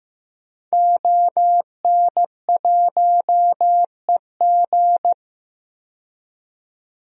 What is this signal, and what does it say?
Morse code, heard as a keyed tone, about 15 wpm, 700 Hz. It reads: ON1EG
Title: DXCC-100 Call Signs - 5 Characters - 15wpm